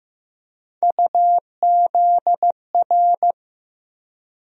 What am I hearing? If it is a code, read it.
Morse code, heard as a keyed tone, about 15 wpm, 700 Hz. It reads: UZR
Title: USA Call Sign Postfixes - 3 Letters 15wpm